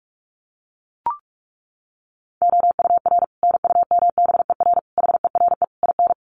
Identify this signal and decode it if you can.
Morse code, heard as a keyed tone, about 44 wpm, 700 Hz. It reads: OUR NUMBER HERE IN
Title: Sets of 5 Words Encore2 45wpm